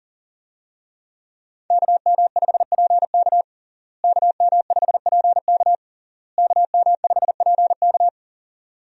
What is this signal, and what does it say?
Morse code, heard as a keyed tone, about 40 wpm, 700 Hz. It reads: KM5PK KM5PK KM5PK
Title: USA Call Signs - 5 Characters - Repeated 3x - 40wpm